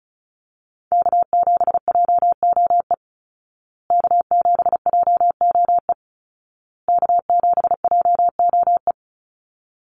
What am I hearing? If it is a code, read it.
Morse code, heard as a keyed tone, about 34 wpm, 700 Hz. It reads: K7JOE K7JOE K7JOE
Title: Call Signs Repeated 2x - 35wpm